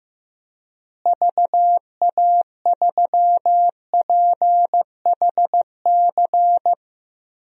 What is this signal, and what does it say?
Morse code, heard as a keyed tone, about 16 wpm, 700 Hz. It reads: VA3PHC
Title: DXCC-100 Call Signs - 6 Characters - Repeated 3x - 15wpm